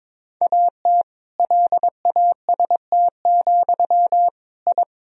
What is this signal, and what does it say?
Morse code, heard as a keyed tone, about 22 wpm, 700 Hz. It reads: AT LAST, I
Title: Sentences from Top 1000 Words - Review Part 1 - 22wpm